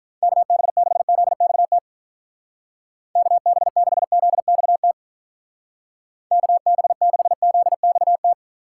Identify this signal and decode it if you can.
Morse code, heard as a keyed tone, about 43 wpm, 700 Hz. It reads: KB6ZXT KB6ZXT KB6ZXT
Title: Speed Racing - USA 2x3 Call Signs 35wpm